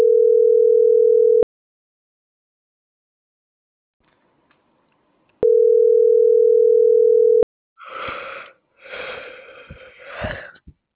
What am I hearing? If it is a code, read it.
Ringback tone (North America), 440 + 480 Hz.